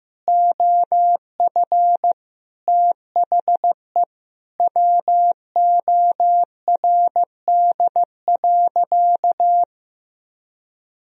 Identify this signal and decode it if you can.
Morse code, heard as a keyed tone, about 16 wpm, 700 Hz. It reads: OF THE WORD.
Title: Sentences from Top 1000 Words - Review Part 4 - 15wpm